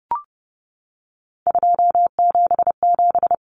Morse code, heard as a keyed tone, about 30 wpm, 700 Hz. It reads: 277
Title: Common Cut Numbers - 3-Digit 30wpm